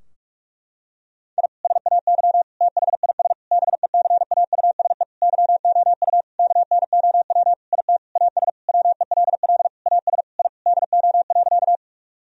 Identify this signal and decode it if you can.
Morse code, heard as a keyed tone, about 41 wpm, 700 Hz. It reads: I SAY THIS BECAUSE YOU KNOW IT AS WELL AS I DO.
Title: Sentences from Top 1000 Words - Review Part 3 - 45wpm